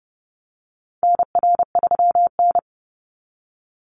Morse code, heard as a keyed tone, about 30 wpm, 700 Hz. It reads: NR3N
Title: Call Signs Repeated 3x - 30wpm